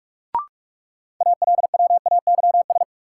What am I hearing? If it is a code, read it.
Morse code, heard as a keyed tone, about 46 wpm, 700 Hz. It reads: ALWAYS
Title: Sentences from Top 300 Words - New Word Drill - 45wpm